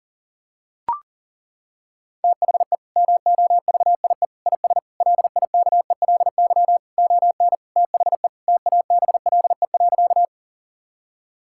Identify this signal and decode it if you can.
Morse code, heard as a keyed tone, about 37 wpm, 700 Hz. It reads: THE MOVIE IS LIKELY ON THE TABLE.
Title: Sentences from Top 1000 Words - Review Part 5 - 40wpm